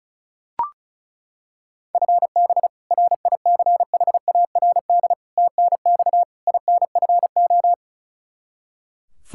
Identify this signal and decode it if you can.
Morse code, heard as a keyed tone, about 36 wpm, 700 Hz. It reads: FB RICHARD TNX INFO
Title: Practice QSO - Exchange 3 - More Info - 35wpm